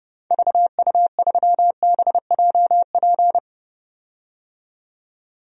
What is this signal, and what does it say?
Morse code, heard as a keyed tone, about 32 wpm, 700 Hz. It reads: VU3BJP
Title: DXCC-100 Call Signs - 6 Characters - 30wpm